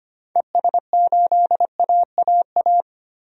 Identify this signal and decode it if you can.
Morse code, heard as a keyed tone, about 25 wpm, 700 Hz. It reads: ES8AAA
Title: DXCC-100 Call Signs - 6 Characters - 25wpm